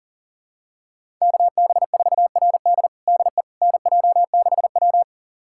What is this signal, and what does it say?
Morse code, heard as a keyed tone, about 40 wpm, 700 Hz. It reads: KB4RD DE NJ6W
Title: USA Callsign DE Callsign 40wpm